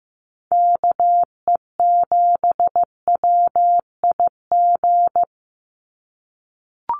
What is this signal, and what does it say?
Morse code, heard as a keyed tone, about 15 wpm, 700 Hz. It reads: KE7WIG
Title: USA Call Signs 15wpm